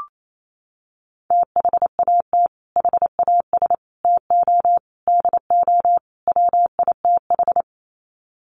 Morse code, heard as a keyed tone, about 28 wpm, 700 Hz. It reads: THAT HAS TO DO WITH
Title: Sets of 5 Words Encore2 28wpm